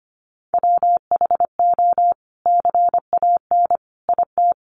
Morse code, heard as a keyed tone, about 25 wpm, 700 Hz. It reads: WHO CAN IT
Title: Sentences from Top 100 Words 25wpm